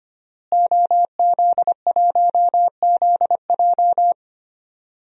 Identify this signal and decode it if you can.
Morse code, heard as a keyed tone, about 25 wpm, 700 Hz. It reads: OZ1ZJ
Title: DXCC-100 Call Signs - 5 Characters - Repeated 3x - 25wpm